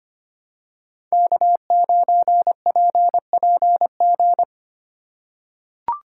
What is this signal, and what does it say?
Morse code, heard as a keyed tone, about 25 wpm, 700 Hz. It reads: K9PPG
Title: USA Call Signs - 5 Characters - 4x Character Spacing - 25wpm